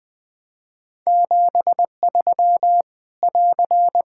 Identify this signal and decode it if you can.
Morse code, heard as a keyed tone, about 20 wpm, 700 Hz. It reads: 73 <AR>
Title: Practice QSO - Exchange 4 - Ending - 20wpm